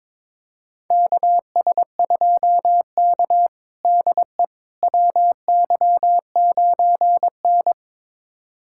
Morse code, heard as a keyed tone, about 22 wpm, 700 Hz. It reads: KS2K DE WY9N
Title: USA Callsign DE Callsign - 4 Characters - 22wpm